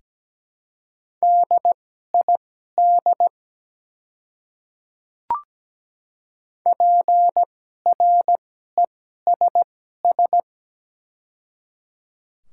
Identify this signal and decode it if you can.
Morse code, heard as a keyed tone, about 17 wpm, spaced as for 12 wpm, 700 Hz. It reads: DID PRESS
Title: 2x Character Spacing - Top 300 Words - 17wpm